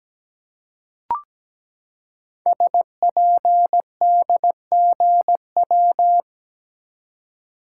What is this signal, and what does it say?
Morse code, heard as a keyed tone, about 17 wpm, 700 Hz. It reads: SPDGW
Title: Code Groups - 5 Letters - 17wpm